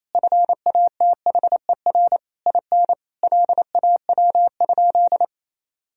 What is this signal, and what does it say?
Morse code, heard as a keyed tone, about 28 wpm, 700 Hz. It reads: FATHER IN LAW?
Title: Sentences from Top 1000 Words - Review Part 4 - 28wpm